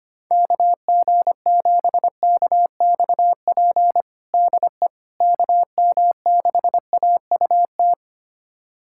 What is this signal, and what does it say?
Morse code, heard as a keyed tone, about 25 wpm, 700 Hz. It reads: KG7KXP DE KM6AUT
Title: USA Callsign DE Callsign - 6 Characters - 25wpm